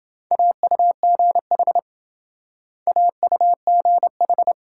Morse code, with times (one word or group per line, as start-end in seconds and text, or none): 0.3-1.8 AUGH
2.9-4.5 AUGH